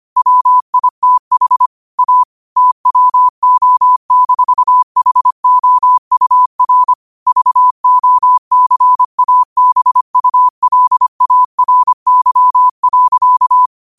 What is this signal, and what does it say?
Morse code, heard as a keyed tone, about 25 wpm, 1000 Hz. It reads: WITH A TWO-HOUR VOCABULARY.
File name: qotdfiles / 25wpm